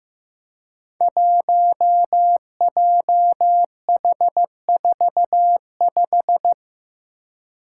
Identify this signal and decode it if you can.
Morse code, heard as a keyed tone, about 15 wpm, 700 Hz. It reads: 1JH45
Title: Five Challenging Letter-Number Combinations 15wpm